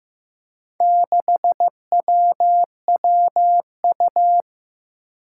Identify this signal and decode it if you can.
Morse code, heard as a keyed tone, about 15 wpm, 700 Hz. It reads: 6WWU
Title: Four Challenging Letter-Number Combinations 15wpm